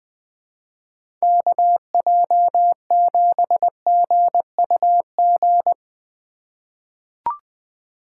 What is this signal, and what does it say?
Morse code, heard as a keyed tone, about 20 wpm, 700 Hz. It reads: KJ7GUG
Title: USA Call Signs - 2x Character Spacing - 20wpm